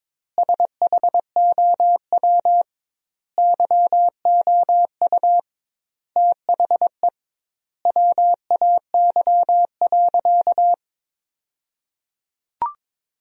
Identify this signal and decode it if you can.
Morse code, heard as a keyed tone, about 22 wpm, 700 Hz. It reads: SHOW YOU THE WAY.
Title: 2x Word-Spacing - Sentences from Top 300 Words - 22wpm